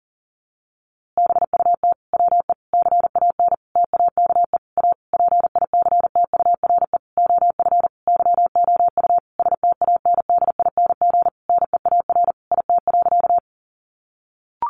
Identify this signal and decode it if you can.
Morse code, heard as a keyed tone, about 37 wpm, 700 Hz. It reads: BUT WE CAN TAKE A PICTURE OF YOU STANDING NEAR IT.
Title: A Sound Of Thunder 40wpm